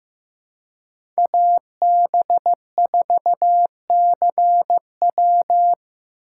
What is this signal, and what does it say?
Morse code, heard as a keyed tone, about 15 wpm, 700 Hz. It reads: AB4CW
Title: USA Call Signs - 5 Characters - 4x Character Spacing - 15wpm